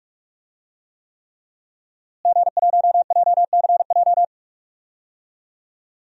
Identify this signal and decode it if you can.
Morse code, heard as a keyed tone, about 42 wpm, 700 Hz. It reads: G1JCJ
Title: DXCC-100 Call Signs - 5 Characters - Repeated 3x - 45wpm